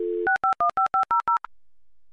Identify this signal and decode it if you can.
DTMF, heard as telephone touch tones, tones of about 120 ms. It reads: 6515500